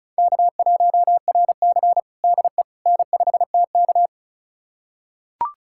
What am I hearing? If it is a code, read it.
Morse code, heard as a keyed tone, about 34 wpm, 700 Hz. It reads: K1RC DE N5TK
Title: USA Callsign DE Callsign - 4 Characters - 35wpm